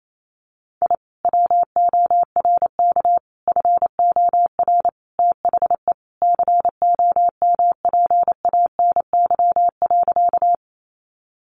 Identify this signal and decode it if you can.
Morse code, heard as a keyed tone, about 28 wpm, 700 Hz. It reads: I WORK FOR THE COMPANY.
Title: Sentences from Top 600 Words 28wpm